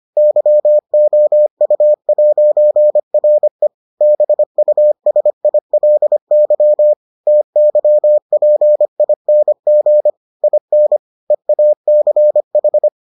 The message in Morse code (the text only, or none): YOU'RE BUSILY TYPING IN EACH